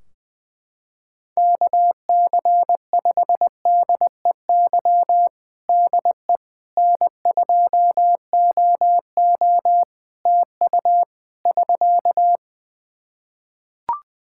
Morse code, heard as a keyed tone, about 21 wpm, 700 Hz. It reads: KC5DEY DE N2OO TU <SK>
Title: K1USN SST QSO Practice 20wpm